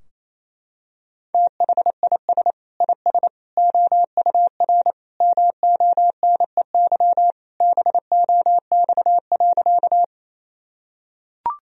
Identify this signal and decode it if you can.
Morse code, heard as a keyed tone, about 28 wpm, 700 Hz. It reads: THIS IS OUR MONEY BOX.
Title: Sentences from Top 1000 Words - Review Part 2 - 28wpm